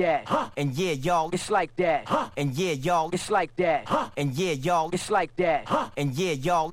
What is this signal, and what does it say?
Speech. There is no code.